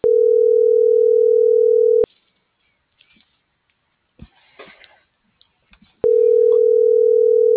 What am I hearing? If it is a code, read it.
Ringback tone (North America), 440 + 480 Hz.